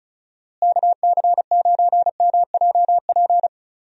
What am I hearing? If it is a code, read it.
Morse code, heard as a keyed tone, about 36 wpm, 700 Hz. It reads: KC9MJP
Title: USA Call Signs - 6 Characters - Repeated 2x - 35wpm